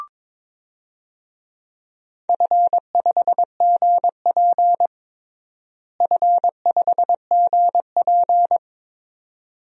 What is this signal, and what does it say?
Morse code, heard as a keyed tone, about 21 wpm, 700 Hz. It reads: F5GP F5GP
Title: DXCC-100 Call Signs - 4 Characters - Repeated 2x - 22wpm